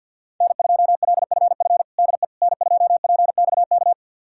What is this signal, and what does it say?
Morse code, heard as a keyed tone, about 51 wpm, 690 Hz. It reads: N2LFF DE N2PXK
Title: USA Callsign DE Callsign - 5 Characters - 50wpm